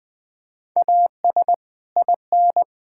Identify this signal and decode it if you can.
Morse code, heard as a keyed tone, about 20 wpm, 700 Hz. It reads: AS IN
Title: Build A Phrase - 5-Word Phrases - 20wpm